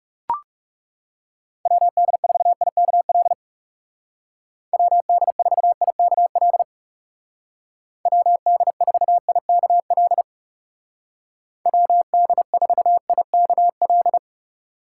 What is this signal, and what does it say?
Morse code, heard as a keyed tone, about 40 wpm, 700 Hz. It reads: WD4IKL WD4IKL WD4IKL WD4IKL